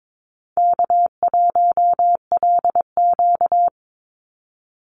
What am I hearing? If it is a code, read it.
Morse code, heard as a keyed tone, about 22 wpm, 700 Hz. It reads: K1LQ
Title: USA Call Signs 22wpm